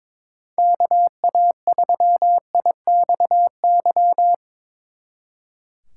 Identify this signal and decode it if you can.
Morse code, heard as a keyed tone, about 25 wpm, 700 Hz. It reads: KA3IXY
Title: Speed Racing - USA 2x3 Call Signs 22wpm